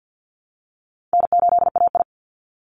Morse code, heard as a keyed tone, about 50 wpm, 700 Hz. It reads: N8AI
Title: Call Signs Repeated 3x - 50wpm